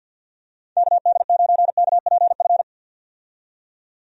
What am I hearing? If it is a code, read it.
Morse code, heard as a keyed tone, about 49 wpm, 700 Hz. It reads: KD9KPF
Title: USA Call Signs - 6 Characters - 50wpm